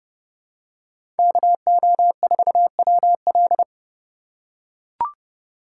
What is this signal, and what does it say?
Morse code, heard as a keyed tone, about 30 wpm, 700 Hz. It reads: KO4WL
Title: USA Call Signs - 5 Characters - 4x Character Spacing - 30wpm